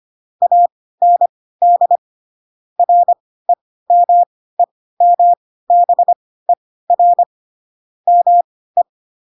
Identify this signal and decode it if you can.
Morse code, heard as a keyed tone, about 25 wpm, spaced as for 16 wpm, 700 Hz. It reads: AND REMEMBER ME